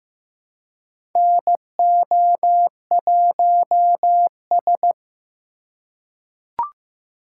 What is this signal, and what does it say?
Morse code, heard as a keyed tone, about 15 wpm, 700 Hz. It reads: NO1S
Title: USA Call Signs - 4 Characters - 2x Character Spacing - 15wpm